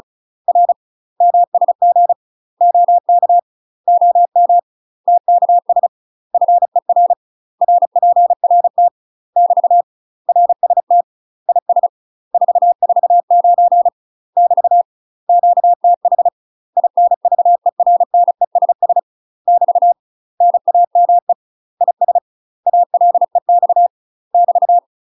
Morse code, heard as a keyed tone, about 33 wpm, 700 Hz. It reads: R MSG OK OM TKS FER RPRT <BT> RST IS 449 <BT> QTH INVERNESS <BT> NAME IS ALEX <BT>
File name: UK MASTER QSOs TRAINING / 35WPM / UK MASTER QSOs 0002